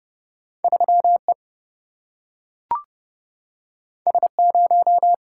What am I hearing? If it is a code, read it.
Morse code, heard as a keyed tone, about 30 wpm, 700 Hz. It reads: 3E S0
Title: Single-Letter Single-Number 30wpm